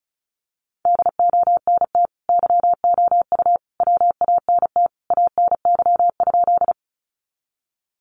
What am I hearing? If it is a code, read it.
Morse code, heard as a keyed tone, about 34 wpm, 700 Hz. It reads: DONT YOU WANT ANY?
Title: Sentences from Top 300 Words 35wpm